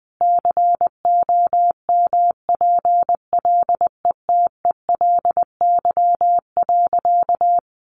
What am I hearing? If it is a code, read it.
Morse code, heard as a keyed tone, about 19 wpm, 700 Hz. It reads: COMPLETELY.
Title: Sentences from Top 1400 Words 20wpm